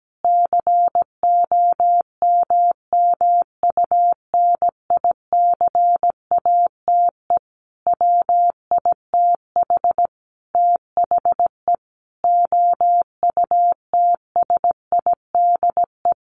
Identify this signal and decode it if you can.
Morse code, heard as a keyed tone, about 17 wpm, 700 Hz. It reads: COMMUNICATE WITH THE OUTSIDE
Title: Sentences from Top 1300 Words 17wpm